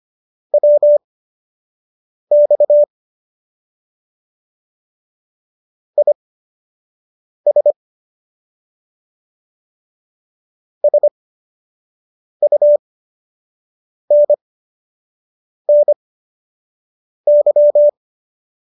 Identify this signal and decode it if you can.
Morse code, heard as a keyed tone, about 25 wpm, spaced as for 6 wpm, 590 Hz. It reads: WX IS SUNNY